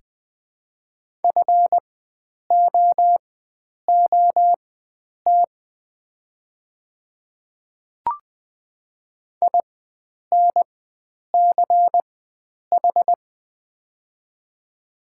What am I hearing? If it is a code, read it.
Morse code, heard as a keyed tone, about 20 wpm, spaced as for 9 wpm, 700 Hz. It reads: FOOT INCH
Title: 4x Character Spacing - Top 500 Words - 20wpm